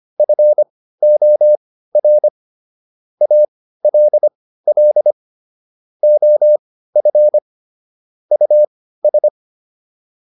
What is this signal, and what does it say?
Morse code, heard as a keyed tone, about 25 wpm, spaced as for 15 wpm, 600 Hz. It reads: FOR ALL OF US